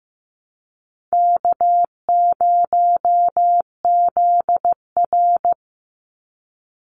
Morse code, heard as a keyed tone, about 15 wpm, 700 Hz. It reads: K0ZR
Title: Call Signs Repeated 3x - 15wpm